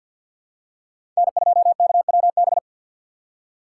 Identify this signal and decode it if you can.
Morse code, heard as a keyed tone, about 52 wpm, 700 Hz. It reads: N2KWB
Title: USA Call Signs - 5 Characters - 4x Character Spacing - 50wpm